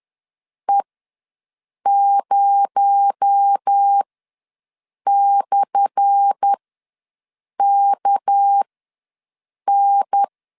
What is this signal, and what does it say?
Morse code, heard as a keyed tone, about 10 wpm, spaced as for 6 wpm, 800 Hz. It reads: E0/KN
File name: LEZIONI / 10 SIGNS / 10A SLASH ws 10wpm 4121